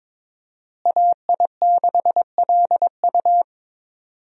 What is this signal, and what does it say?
Morse code, heard as a keyed tone, about 21 wpm, 700 Hz. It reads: AI6LU
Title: USA Call Signs - 5 Characters - Repeated 3x - 22wpm